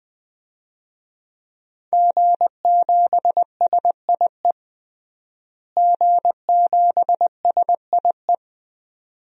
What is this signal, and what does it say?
Morse code, heard as a keyed tone, about 20 wpm, 700 Hz. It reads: G7SIE G7SIE
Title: DXCC-100 Call Signs - 5 Characters - Repeated 2x - 20wpm